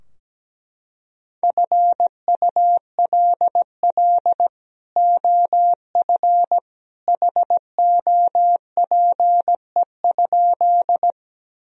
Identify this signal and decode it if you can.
Morse code, heard as a keyed tone, about 17 wpm, 700 Hz. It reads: FULL OF HOPE?
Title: Sentences from Top 1000 Words - Review Part 3 - 17wpm